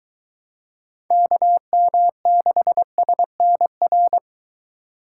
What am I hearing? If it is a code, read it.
Morse code, heard as a keyed tone, about 23 wpm, 700 Hz. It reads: KM6SNR